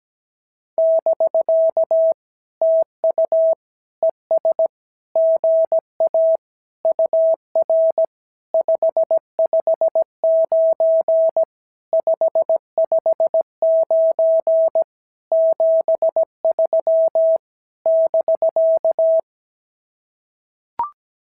Morse code, heard as a keyed tone, about 17 wpm, 650 Hz. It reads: <BK> TU ES GA UR 559 559 73 <BK>
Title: SOTA QSO Practice 17wpm